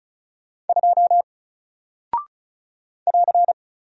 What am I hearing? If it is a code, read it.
Morse code, heard as a keyed tone, about 36 wpm, 700 Hz. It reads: 2 <AR>
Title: Single Character (Letter:Number:Punctuation:Prosign) 35wpm